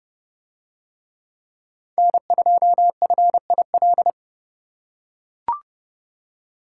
Morse code, heard as a keyed tone, about 30 wpm, 700 Hz. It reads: N2FIL